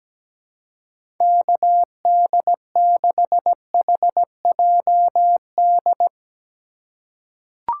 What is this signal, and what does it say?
Morse code, heard as a keyed tone, about 17 wpm, 700 Hz. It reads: KD6HJD
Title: USA Call Signs - 6 Characters - 3x Character Spacing - 17wpm